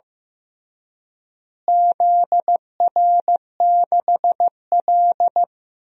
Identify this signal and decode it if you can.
Morse code, heard as a keyed tone, about 15 wpm, 700 Hz. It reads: ZR6L
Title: DXCC-100 Call Signs - 4 Characters - Repeated 2x - 15wpm